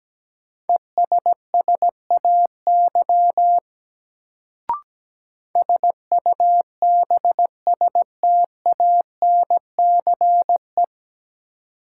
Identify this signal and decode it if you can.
Morse code, heard as a keyed tone, about 17 wpm, 700 Hz. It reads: ESSAY SUBSTANCE
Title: Sentences from Top 1500 Words - New Word Drill - 17wpm